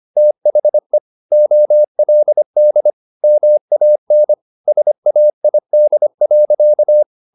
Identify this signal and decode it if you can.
Morse code, heard as a keyed tone, about 25 wpm, 600 Hz. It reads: THE OLD MAN SAID.